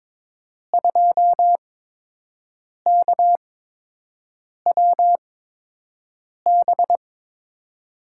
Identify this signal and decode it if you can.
Morse code, heard as a keyed tone, about 22 wpm, spaced as for 6 wpm, 700 Hz. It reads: 2KWB
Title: USA Call Signs - 5 Characters - 8x Character Spacing - 22wpm